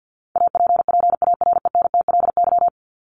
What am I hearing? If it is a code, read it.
Morse code, heard as a keyed tone, about 50 wpm, 700 Hz. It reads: APPARENTLY
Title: LY Suffix - Top 100 Words with Base Words 50wpm